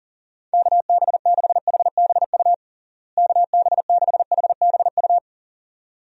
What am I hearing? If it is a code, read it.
Morse code, heard as a keyed tone, about 40 wpm, 700 Hz. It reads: KB6HBU KB6HBU
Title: USA Call Signs - 6 Characters - Repeated 2x - 40wpm